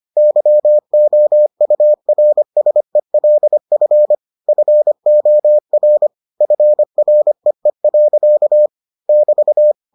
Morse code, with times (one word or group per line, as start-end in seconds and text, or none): 0.2-4.2 YOURSELF
4.5-6.1 FOR
6.4-8.7 FREE.
9.1-9.7 <BT>